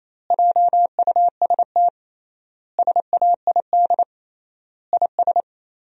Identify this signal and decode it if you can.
Morse code, heard as a keyed tone, about 28 wpm, 700 Hz. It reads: JUST SAID IS
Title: 3x Word-Spacing - Sentences from Top 200 Words - 28wpm